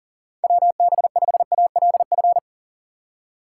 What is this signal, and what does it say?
Morse code, heard as a keyed tone, about 38 wpm, 700 Hz. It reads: WB5ALF
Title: USA Call Signs - 6 Characters - 2x Character Spacing - 40wpm